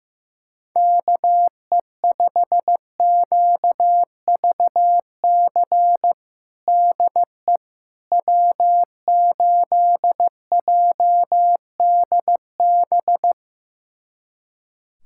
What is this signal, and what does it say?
Morse code, heard as a keyed tone, about 15 wpm, 700 Hz. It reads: KE5QVC DE W8JDB
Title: USA Callsign DE Callsign 15wpm